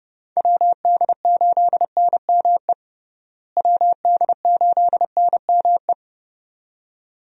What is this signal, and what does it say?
Morse code, heard as a keyed tone, about 30 wpm, 700 Hz. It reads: WD8NME WD8NME
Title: USA Call Signs - 6 Characters - Repeated 2x - 30wpm